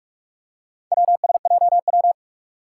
Morse code, heard as a keyed tone, about 44 wpm, 700 Hz. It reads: WSJW
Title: Four Challenging Letter-Number Combinations 45wpm